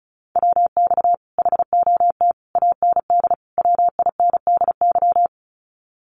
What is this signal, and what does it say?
Morse code, heard as a keyed tone, about 34 wpm, 700 Hz. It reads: WX HOT AND WINDY
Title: Common QSO Phrases Encore 35wpm